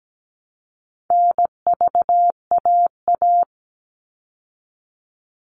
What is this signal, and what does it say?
Morse code, heard as a keyed tone, about 17 wpm, 700 Hz. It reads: NVAA
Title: All Cut Numbers - 4-Digit 17wpm